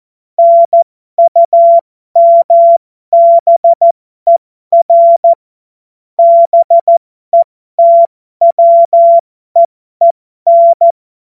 Morse code, heard as a keyed tone, about 14 wpm, spaced as for 12 wpm, 680 Hz. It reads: NUMBER BETWEEN
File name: Morsekurs / Witze